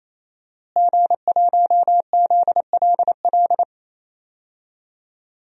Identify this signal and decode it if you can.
Morse code, heard as a keyed tone, about 27 wpm, 700 Hz. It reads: G1ZLL
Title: DXCC-100 Call Signs - 5 Characters - Repeated 3x - 28wpm